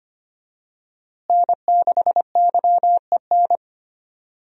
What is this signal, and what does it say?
Morse code, heard as a keyed tone, about 25 wpm, 700 Hz. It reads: N6YEN
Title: USA Call Signs - 5 Characters - Repeated 3x - 25wpm